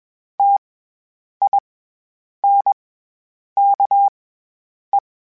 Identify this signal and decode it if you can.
Morse code, heard as a keyed tone, about 21 wpm, spaced as for 8 wpm, 800 Hz. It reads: TINKE
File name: Special INGs-mp3s / Special INGs-mp3s / SlowToFaster ing @20